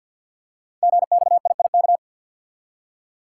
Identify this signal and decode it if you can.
Morse code, heard as a keyed tone, about 53 wpm, 700 Hz. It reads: GXIIK
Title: Code Groups - 5 Letters - 50wpm